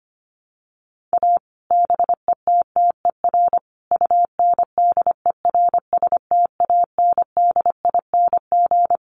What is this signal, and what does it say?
Morse code, heard as a keyed tone, about 25 wpm, 700 Hz. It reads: A BETTER UNDERSTANDING
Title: Sets of 5 Words Encore2 25wpm